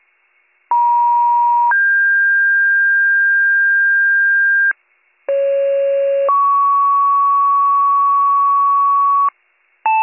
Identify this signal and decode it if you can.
Two-tone page, heard as a one-second tone, then a three-second tone, about 950 then 1600 Hz, about 550 then 1120 Hz, with noise. It is heard from 0.7 to 9.3 s.